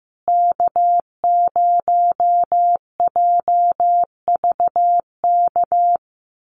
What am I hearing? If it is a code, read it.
Morse code, heard as a keyed tone, about 15 wpm, 700 Hz. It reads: K0JVK
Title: USA Call Signs Repeated 3x 15wpm